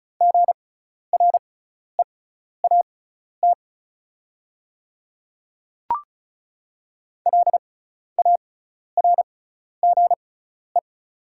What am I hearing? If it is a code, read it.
Morse code, heard as a keyed tone, about 36 wpm, spaced as for 12 wpm, 690 Hz. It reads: GREAT LARGE
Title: 6x Character Spacing - Top 200 Words - 35wpm